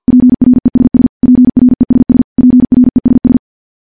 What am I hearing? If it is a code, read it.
Morse code, heard as a keyed tone, about 50 wpm, 250 Hz. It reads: PRESS PRESS PRESS
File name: morse / morse-training-50WPM